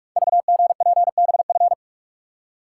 Morse code, heard as a keyed tone, about 44 wpm, 690 Hz. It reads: VGPBF
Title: Code Groups - 5 Letters - 45wpm